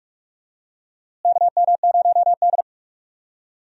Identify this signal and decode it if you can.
Morse code, heard as a keyed tone, about 46 wpm, 700 Hz. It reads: KM0D
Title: USA Call Signs - 4 Characters - Repeated 3x - 45wpm